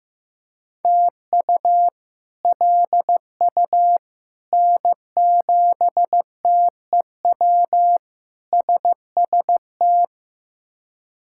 Morse code, heard as a keyed tone, about 15 wpm, 700 Hz. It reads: TU LU N7TEW SST
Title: K1USN SST QSO Practice - 15wpm